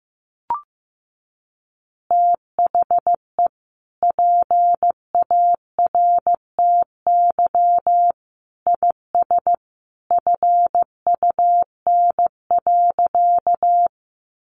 Morse code, heard as a keyed tone, about 15 wpm, 700 Hz. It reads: THE PARTY IS FUN.
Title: Easy Phrases 15wpm